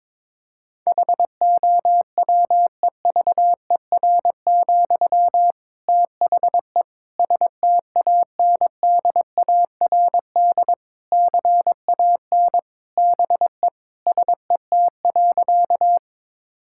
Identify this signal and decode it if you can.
Morse code, heard as a keyed tone, about 22 wpm, 700 Hz. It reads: HOWEVER, THE STANDARD CAN BE SET.
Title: Sentences from Top 1000 Words - Review Part 6 - 22wpm